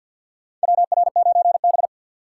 Morse code, heard as a keyed tone, about 50 wpm, 700 Hz. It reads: WR9B
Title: USA Call Signs - 2x Character Spacing - 50wpm